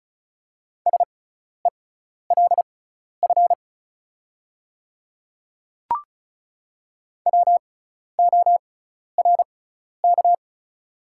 Morse code, heard as a keyed tone, about 36 wpm, spaced as for 12 wpm, 690 Hz. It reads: SELF WORK
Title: 6x Character Spacing - Top 300 Words - 35wpm